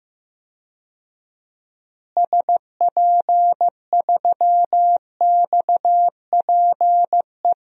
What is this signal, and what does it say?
Morse code, heard as a keyed tone, about 15 wpm, 700 Hz. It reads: SP3XPE
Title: DXCC-100 Call Signs - 6 Characters - Repeated 2x - 15wpm